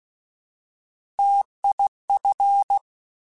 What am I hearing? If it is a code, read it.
Morse code, heard as a keyed tone, about 16 wpm, 800 Hz. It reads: TIF